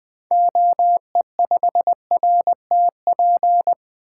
Morse code, heard as a keyed tone, about 21 wpm, 700 Hz. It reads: OE5RTP
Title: 2023 SOTA Activator Call Signs - 20wpm